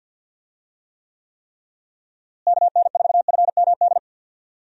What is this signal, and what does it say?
Morse code, heard as a keyed tone, about 50 wpm, 700 Hz. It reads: KN4FMD
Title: USA Call Signs - 6 Characters - Repeated 3x - 50wpm